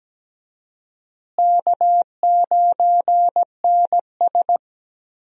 Morse code, heard as a keyed tone, about 17 wpm, 700 Hz. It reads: K9NS
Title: USA Call Signs - 4 Characters - Repeated 3x - 17wpm